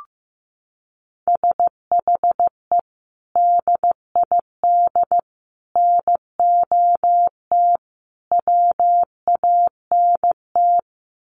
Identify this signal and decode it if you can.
Morse code, heard as a keyed tone, about 15 wpm, 700 Hz. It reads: SHE DID NOT WANT
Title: Sets of 4 Words 15wpm